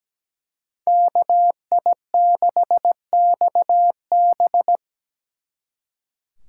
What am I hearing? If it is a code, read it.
Morse code, heard as a keyed tone, about 20 wpm, 700 Hz. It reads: KI6XB
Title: Speed Racing - USA 2x2 Call Signs 17wpm